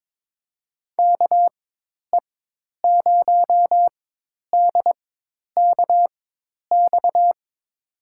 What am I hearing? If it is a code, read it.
Morse code, heard as a keyed tone, about 22 wpm, spaced as for 10 wpm, 700 Hz. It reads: KE0DKX
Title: USA Call Signs - 6 Characters - 4x Character Spacing - 22wpm